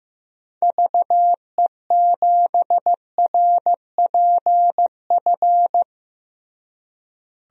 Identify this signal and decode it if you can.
Morse code, heard as a keyed tone, about 15 wpm, 700 Hz. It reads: VE7RPF
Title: DXCC-100 Call Signs - 6 Characters - Repeated 2x - 15wpm